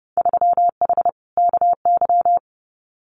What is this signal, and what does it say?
Morse code, heard as a keyed tone, about 30 wpm, 700 Hz. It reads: 3H KY
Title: Winter Field Day Exchanges Encore 30wpm